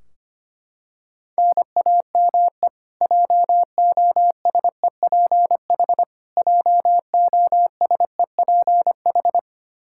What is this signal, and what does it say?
Morse code, heard as a keyed tone, about 25 wpm, 700 Hz. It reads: NAME JOSEPH JOSEPH
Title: Practice QSO - Exchange 1 - Beginning - 25wpm